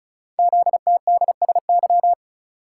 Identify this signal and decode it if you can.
Morse code, heard as a keyed tone, about 36 wpm, 690 Hz. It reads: ZTDSY
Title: Code Groups - 5 Letters - 35wpm